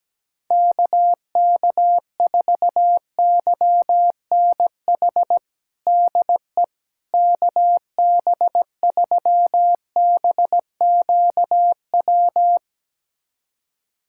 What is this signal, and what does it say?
Morse code, heard as a keyed tone, about 17 wpm, 700 Hz. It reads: KK4YNH DE KB3BQW
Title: USA Callsign DE Callsign - 6 Characters - 17wpm